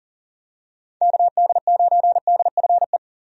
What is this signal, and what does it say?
Morse code, heard as a keyed tone, about 40 wpm, 700 Hz. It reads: KD9DFE
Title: USA Call Signs - 6 Characters - Repeated 2x - 40wpm